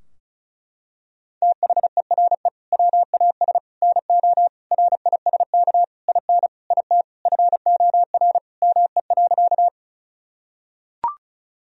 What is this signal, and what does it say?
Morse code, heard as a keyed tone, about 36 wpm, 700 Hz. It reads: THERE WAS NO RISK IN IT FOR ME.
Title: Sentences from Top 1000 Words - Review Part 4 - 35wpm